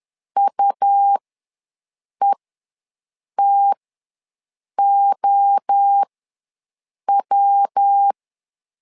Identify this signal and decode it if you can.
Morse code, heard as a keyed tone, about 10 wpm, spaced as for 6 wpm, 800 Hz. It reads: UETOW
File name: LEZIONI / 3 MWRO / 3 MWRO ws 10wpm 2310